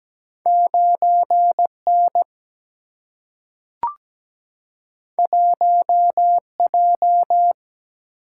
Morse code, heard as a keyed tone, about 17 wpm, 700 Hz. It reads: 9 N 1 J
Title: Single-Letter Single-Number 17wpm